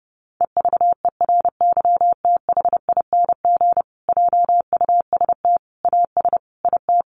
Morse code, heard as a keyed tone, about 30 wpm, 700 Hz. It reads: EVERYTHING JUST AS IT
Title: Sentences from Top 700 Words 30wpm